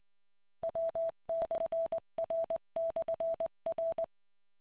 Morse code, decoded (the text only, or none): W/R/R